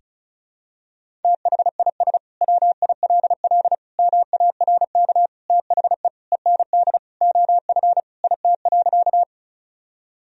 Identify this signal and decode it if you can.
Morse code, heard as a keyed tone, about 34 wpm, 700 Hz. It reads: THIS WILL MARK THE END OF IT.
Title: Sentences from Top 1000 Words - Review Part 1 - 35wpm